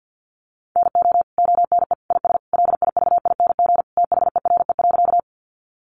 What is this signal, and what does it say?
Morse code, heard as a keyed tone, about 50 wpm, 700 Hz. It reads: NO ONE IS LIVING THERE.
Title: Sentences from Top 1100 Words 50wpm